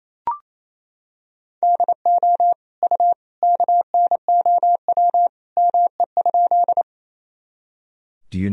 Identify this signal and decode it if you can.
Morse code, heard as a keyed tone, about 28 wpm, 700 Hz. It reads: DO U KNOW ME?
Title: Sentences from Top 200 Words and Abbreviations 28wpm